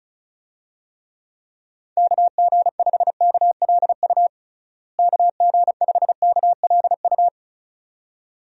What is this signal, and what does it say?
Morse code, heard as a keyed tone, about 34 wpm, 700 Hz. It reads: KG5KLU KG5KLU
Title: USA Call Signs - 6 Characters - Repeated 2x - 35wpm